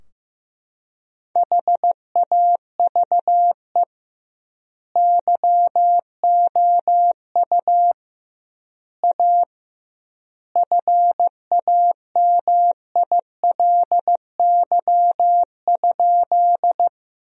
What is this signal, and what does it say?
Morse code, heard as a keyed tone, about 15 wpm, 700 Hz. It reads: HAVE YOU A FAMILY?
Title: 2x Word-Spacing - Sentences from Top 400 Words - 15wpm